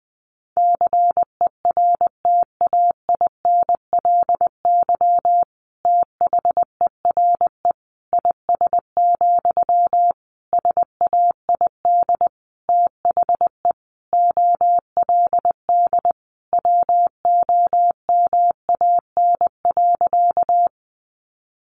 Morse code, heard as a keyed tone, about 20 wpm, 700 Hz. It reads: CERTAINLY THERE IS, SAID THE OLD WOMAN.
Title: Sentences from Top 1100 Words 20wpm